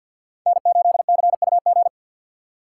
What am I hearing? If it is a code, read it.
Morse code, heard as a keyed tone, about 49 wpm, 700 Hz. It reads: N8CUG
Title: USA Call Signs - 4x Character Spacing - 50wpm